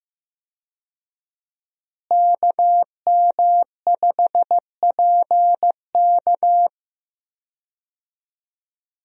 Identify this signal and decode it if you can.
Morse code, heard as a keyed tone, about 15 wpm, 700 Hz. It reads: KM5PK
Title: USA Call Signs - 5 Characters - Repeated 3x - 15wpm